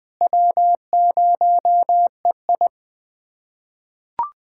Morse code, heard as a keyed tone, about 21 wpm, 700 Hz. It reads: W0EI
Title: USA Callsign DE Callsign - 4 Characters - 20wpm